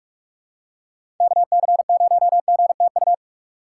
Morse code, heard as a keyed tone, about 46 wpm, 700 Hz. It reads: KC0GTU
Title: USA Call Signs - 6 Characters - 6x Character Spacing - 45wpm